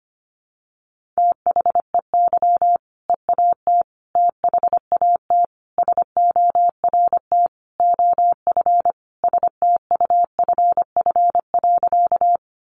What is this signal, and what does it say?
Morse code, heard as a keyed tone, about 25 wpm, 700 Hz. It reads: THEY EAT THAT SORT OF STUFF.
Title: Sentences from Top 1400 Words 25wpm